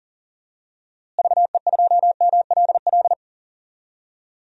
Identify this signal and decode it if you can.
Morse code, heard as a keyed tone, about 40 wpm, 700 Hz. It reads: VE2MLL
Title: DXCC-100 Call Signs - 6 Characters - Repeated 2x - 40wpm